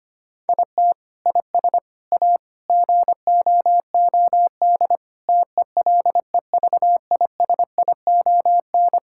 Morse code, heard as a keyed tone, about 25 wpm, 700 Hz. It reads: IT IS A GOOD TELEVISION
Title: Sentences from Top 1000 Words - Review Part 6 - 25wpm